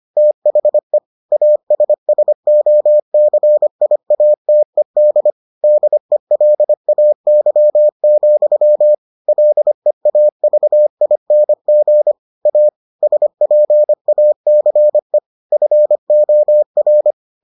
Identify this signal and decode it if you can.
Morse code, heard as a keyed tone, about 25 wpm, 600 Hz. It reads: THE ASSOCIATED DELAY, LEAVING A SPACE FOR